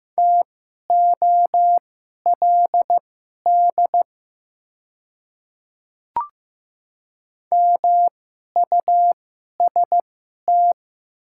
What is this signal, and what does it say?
Morse code, heard as a keyed tone, about 15 wpm, spaced as for 11 wpm, 690 Hz. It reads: TOLD MUST
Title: 2x Character Spacing - Top 500 Words - 15wpm